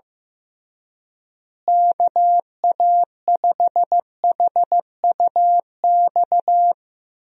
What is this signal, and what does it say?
Morse code, heard as a keyed tone, about 15 wpm, 700 Hz. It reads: KA5HUX
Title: USA Call Signs - 6 Characters - Repeated 2x - 15wpm